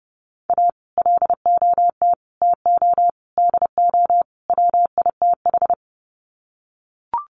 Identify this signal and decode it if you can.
Morse code, heard as a keyed tone, about 30 wpm, 700 Hz. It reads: A LOT TO DO WITH
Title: Sets of 5 Words 30wpm